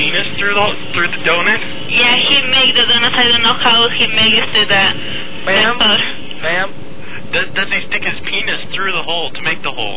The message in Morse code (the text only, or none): none